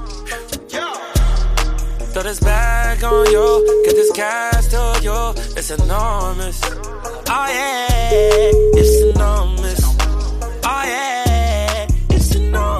Ringback tone (Europe, 425 Hz), with noise.